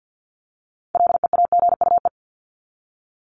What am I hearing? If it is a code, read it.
Morse code, heard as a keyed tone, about 50 wpm, 700 Hz. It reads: LEAGUE
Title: Top 1751-2000 Words 50wpm